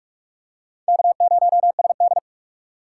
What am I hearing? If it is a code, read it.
Morse code, heard as a keyed tone, about 42 wpm, 700 Hz. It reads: K0SD
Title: USA Call Signs - 4 Characters - 3x Character Spacing - 45wpm